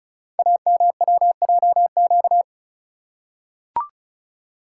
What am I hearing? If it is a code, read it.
Morse code, heard as a keyed tone, about 36 wpm, 700 Hz. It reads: AMWJQ
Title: Code Groups - 5 Letters - 35wpm